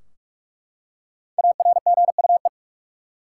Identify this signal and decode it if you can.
Morse code, heard as a keyed tone, about 46 wpm, 700 Hz. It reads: ARGUE
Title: Five Letter Words - 45wpm